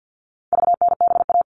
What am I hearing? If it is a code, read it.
Morse code, heard as a keyed tone, about 50 wpm, 700 Hz. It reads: VNBA
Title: All Cut Numbers - 4-Digit 50wpm